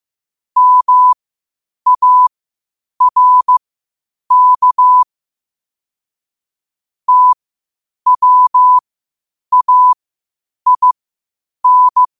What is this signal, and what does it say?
Morse code, heard as a keyed tone, about 15 wpm, spaced as for 8 wpm, 1000 Hz. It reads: MARK TWAIN
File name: qotdfiles / 8wpm